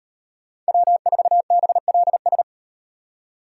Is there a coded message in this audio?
Yes: Morse code from 0.7 to 2.4 s, heard as a keyed tone, about 36 wpm, 700 Hz. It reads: W4BLS